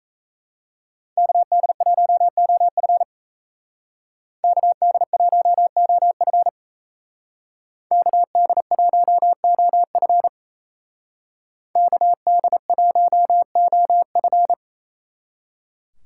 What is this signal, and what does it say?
Morse code, heard as a keyed tone, about 36 wpm, 700 Hz. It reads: KD1OF KD1OF KD1OF KD1OF